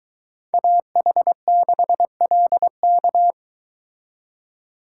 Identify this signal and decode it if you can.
Morse code, heard as a keyed tone, about 23 wpm, 700 Hz. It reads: AH6LK